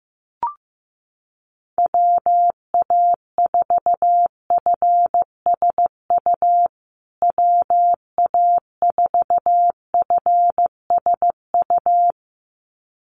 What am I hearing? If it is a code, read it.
Morse code, heard as a keyed tone, about 15 wpm, 700 Hz. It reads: WA4FSU WA4FSU
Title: USA Call Signs Repeated 2x 15wpm